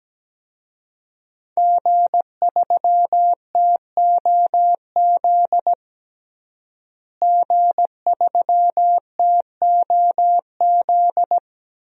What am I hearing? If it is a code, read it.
Morse code, heard as a keyed tone, about 17 wpm, 700 Hz. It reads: G3TOZ G3TOZ
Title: DXCC-100 Call Signs - 5 Characters - Repeated 3x - 17wpm